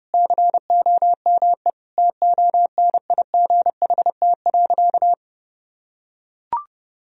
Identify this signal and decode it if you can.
Morse code, heard as a keyed tone, about 32 wpm, 700 Hz. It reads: COME TONIGHT.
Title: Sentences from Top 1000 Words - Review Part 3 - 30wpm